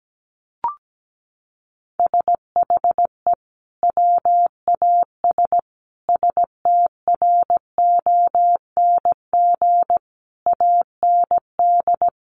Morse code, heard as a keyed tone, about 17 wpm, 700 Hz. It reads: SHE WAS STRONG AND
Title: Sentences from Top 1200 Words 17wpm